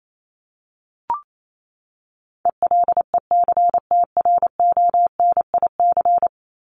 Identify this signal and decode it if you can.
Morse code, heard as a keyed tone, about 28 wpm, 700 Hz. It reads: ELECTRONIC
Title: Top 1001-1250 Words 28wpm